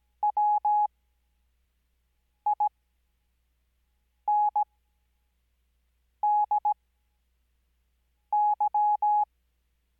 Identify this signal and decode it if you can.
Morse code, heard as a keyed tone, about 17 wpm, spaced as for 5 wpm, 840 Hz. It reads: WINDY
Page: Farnsworth Spacing: 15WPM Character speed Spaced at 5WPM.